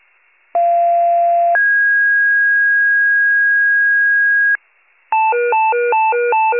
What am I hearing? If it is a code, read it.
Two-tone page, heard as a one-second tone, then a three-second tone, about 690 then 1650 Hz. It is heard from 0.5 to 4.6 s.
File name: pager / 41 RESCUE DUTY / wcalert